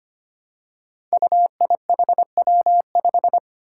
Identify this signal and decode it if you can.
Morse code, heard as a keyed tone, about 25 wpm, 700 Hz. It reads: UIHW5
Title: Five Challenging Letter-Number Combinations 25wpm